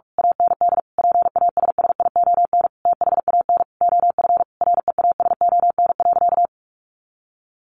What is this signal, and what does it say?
Morse code, heard as a keyed tone, about 44 wpm, 700 Hz. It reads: AND PASSION THAN OF REASON.
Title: Sentences from Top 1300 Words 45wpm